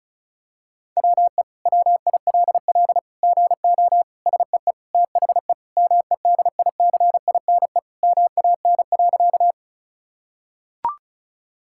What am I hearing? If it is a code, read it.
Morse code, heard as a keyed tone, about 36 wpm, 700 Hz. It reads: WE WILL GO SEE THE MEDICINE MAN.
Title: Sentences from Top 1000 Words - Review Part 6 - 35wpm